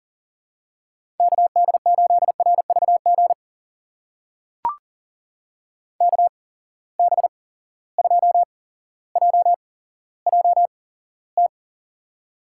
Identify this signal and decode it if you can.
Morse code, heard as a keyed tone, about 40 wpm, spaced as for 17 wpm, 700 Hz. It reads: KD8RVG KB2JJT
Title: USA Call Signs - 6 Characters - 8x Character Spacing - 40wpm